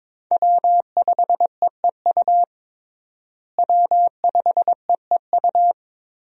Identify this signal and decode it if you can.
Morse code, heard as a keyed tone, about 22 wpm, 700 Hz. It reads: W5EEU W5EEU
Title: USA Call Signs - 5 Characters - Repeated 2x - 22wpm